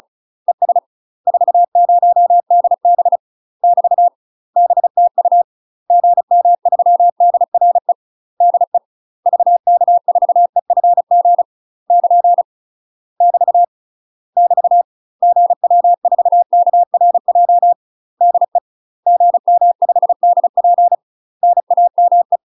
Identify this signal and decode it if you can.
Morse code, heard as a keyed tone, about 34 wpm, 700 Hz. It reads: ES 40DB <BT> BTU GM3DRE DE VK4EFG <KN> <BT> <BT> GW4KRJ DE GM5DP NAME
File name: UK MASTER QSOs TRAINING / 35WPM / UK MASTER QSOs 0000